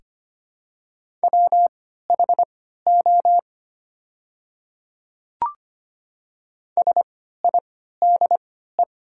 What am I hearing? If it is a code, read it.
Morse code, heard as a keyed tone, about 25 wpm, spaced as for 14 wpm, 700 Hz. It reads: WHO SIDE
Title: 3x Character Spacing - Top 200 Words - 25wpm